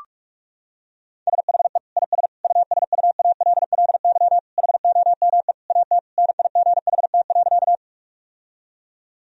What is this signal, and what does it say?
Morse code, heard as a keyed tone, about 46 wpm, 700 Hz. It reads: SHE IS USUALLY HOME AT NIGHT.
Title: Sentences from Top 1000 Words - Review Part 3 - 45wpm